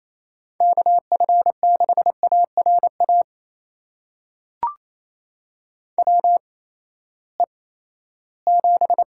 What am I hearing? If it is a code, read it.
Morse code, heard as a keyed tone, about 28 wpm, spaced as for 16 wpm, 700 Hz. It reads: KF6ARA WE7
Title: USA Call Signs - 8x Character Spacing - 28wpm